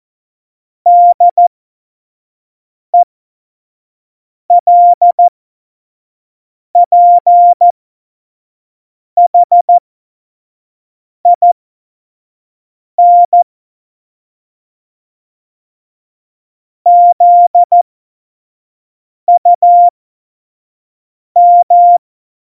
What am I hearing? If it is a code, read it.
Morse code, heard as a keyed tone, about 14 wpm, spaced as for 5 wpm, 700 Hz. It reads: DELPHIN ZUM
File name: Morsekurs / Witze / Witz 11. Dezember 2018, 25 F70